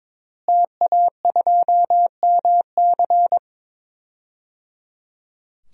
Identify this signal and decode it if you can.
Morse code, heard as a keyed tone, about 22 wpm, 700 Hz. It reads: TA2MC
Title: DXCC-100 Call Signs - 5 Characters - Repeated 3x - 22wpm